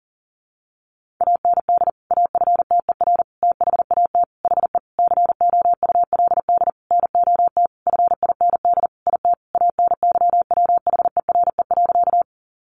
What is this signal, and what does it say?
Morse code, heard as a keyed tone, about 40 wpm, 700 Hz. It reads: AND AFTER THAT HE COULD NOT FIND IT ANYWHERE.
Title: Sentences from Top 1400 Words 40wpm